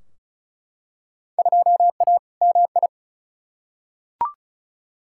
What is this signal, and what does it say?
Morse code, heard as a keyed tone, about 36 wpm, 700 Hz. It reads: 2A MI
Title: Field Day Exchanges - 35wpm